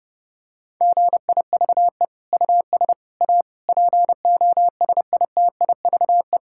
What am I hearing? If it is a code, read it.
Morse code, heard as a keyed tone, about 30 wpm, 700 Hz. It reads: GIVE US A POSITIVE
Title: Sentences from Top 1600 Words 30wpm